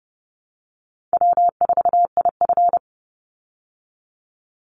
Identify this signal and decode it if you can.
Morse code, heard as a keyed tone, about 30 wpm, 700 Hz. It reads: W4IF
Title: Easy Call Signs Repeated 3x - 30wpm